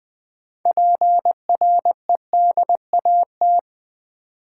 Morse code, heard as a keyed tone, about 20 wpm, 700 Hz. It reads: PREDAT
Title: Build A Word - 2 Letters At A Time - 8-Letter Words - 20wpm